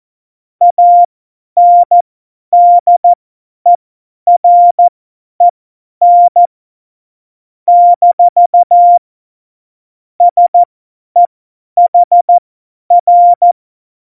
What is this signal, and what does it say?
Morse code, heard as a keyed tone, about 14 wpm, spaced as for 10 wpm, 700 Hz. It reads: ANDEREN - SEHR
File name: Morsekurs / Witze / Witz 05. September 17, Tempo 50